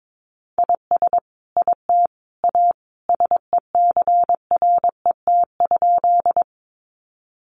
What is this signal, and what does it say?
Morse code, heard as a keyed tone, about 22 wpm, 700 Hz. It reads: IS IT A SECRET?
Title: Sentences from Top 1200 Words 22wpm